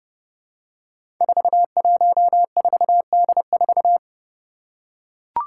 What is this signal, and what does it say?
Morse code, heard as a keyed tone, about 30 wpm, 700 Hz. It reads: 414D4
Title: Five Challenging Letter-Number Combinations 30wpm